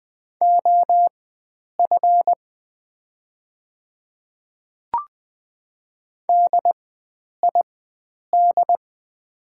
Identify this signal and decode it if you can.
Morse code, heard as a keyed tone, about 20 wpm, spaced as for 9 wpm, 690 Hz. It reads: OF DID
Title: 4x Character Spacing - Top 100 Words - 20wpm